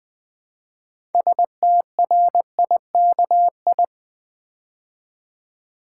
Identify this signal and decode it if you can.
Morse code, heard as a keyed tone, about 20 wpm, 700 Hz. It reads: STRIKI
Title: Build A Word - 2 Letters At A Time - 8-Letter Words - 20wpm